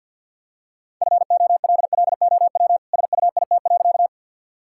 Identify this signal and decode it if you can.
Morse code, heard as a keyed tone, about 50 wpm, 700 Hz. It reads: FOLLOW SUIT.
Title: Sentences from Top 1600 Words 50wpm